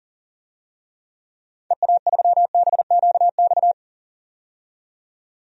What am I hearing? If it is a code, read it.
Morse code, heard as a keyed tone, about 40 wpm, 700 Hz. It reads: EA3BQX
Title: DXCC-100 Call Signs - 6 Characters - 40wpm